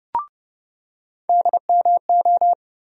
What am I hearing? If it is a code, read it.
Morse code, heard as a keyed tone, about 30 wpm, 700 Hz. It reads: DMO
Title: USA Call Sign Postfixes - 3 Letters 30wpm